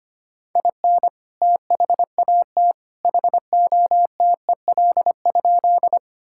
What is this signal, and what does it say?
Morse code, heard as a keyed tone, about 25 wpm, 700 Hz. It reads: IN THAT HOTEL?
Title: Sentences from Top 1600 Words 25wpm